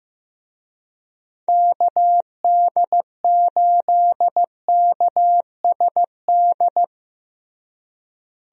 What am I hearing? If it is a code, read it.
Morse code, heard as a keyed tone, about 15 wpm, 700 Hz. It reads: KD8KSD
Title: USA Call Signs - 6 Characters - Repeated 2x - 15wpm